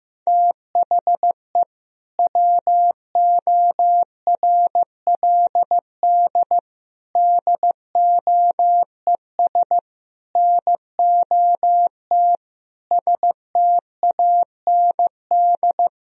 Morse code, heard as a keyed tone, about 16 wpm, 700 Hz. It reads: THE WORLD DOES NOT STAND
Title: Sentences from Top 1000 Words - Review Part 1 - 15wpm